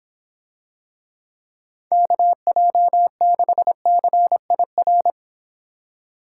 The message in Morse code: KJ6CIR

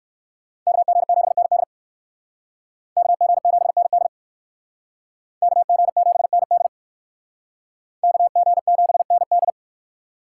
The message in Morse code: KG7ND KG7ND KG7ND KG7ND

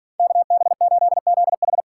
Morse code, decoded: KB8ZH

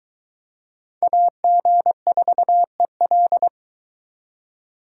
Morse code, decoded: AG4EL

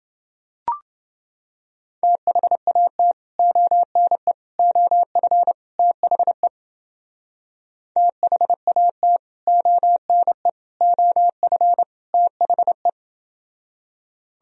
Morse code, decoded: THAT ONE OF THE THAT ONE OF THE

